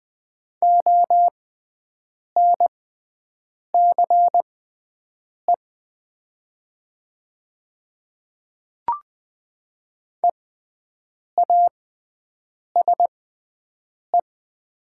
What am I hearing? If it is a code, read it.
Morse code, heard as a keyed tone, about 20 wpm, spaced as for 7 wpm, 700 Hz. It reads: ONCE EASE
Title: 6x Character Spacing - Top 300 Words - 20wpm